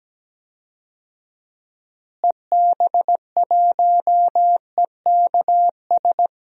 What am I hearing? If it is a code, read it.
Morse code, heard as a keyed tone, about 17 wpm, 700 Hz. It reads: EB1EKS
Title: DXCC-100 Call Signs - 6 Characters - Repeated 2x - 17wpm